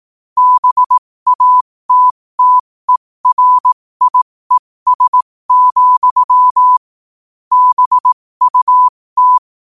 Morse code, heard as a keyed tone, about 18 wpm, spaced as for 15 wpm, 1000 Hz. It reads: BATTERIES, BUT
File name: qotdfiles / 15wpm